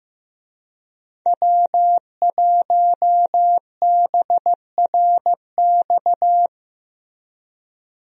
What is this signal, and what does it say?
Morse code, heard as a keyed tone, about 16 wpm, 700 Hz. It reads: W1BRX
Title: USA Call Signs - 5 Characters - Repeated 3x - 15wpm